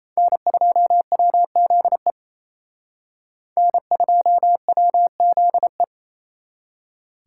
Morse code, heard as a keyed tone, about 34 wpm, 700 Hz. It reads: N2WZE N2WZE
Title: Speed Racing - USA 1x3 Call Signs 28wpm